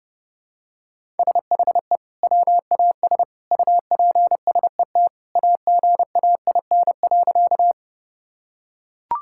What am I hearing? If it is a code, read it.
Morse code, heard as a keyed tone, about 30 wpm, 700 Hz. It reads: SHE WAS UPSET AGAIN.
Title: Sentences from Top 1000 Words - Review Part 5 - 30wpm